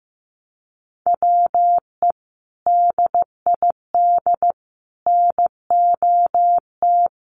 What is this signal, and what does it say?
Morse code, heard as a keyed tone, about 15 wpm, 700 Hz. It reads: WE DID NOT
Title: Sets of 4 Words 15wpm